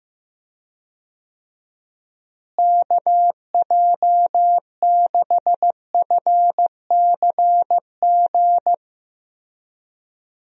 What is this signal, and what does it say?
Morse code, heard as a keyed tone, about 15 wpm, 700 Hz. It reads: KJ6FCG
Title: USA Call Signs - 6 Characters - Repeated 3x - 15wpm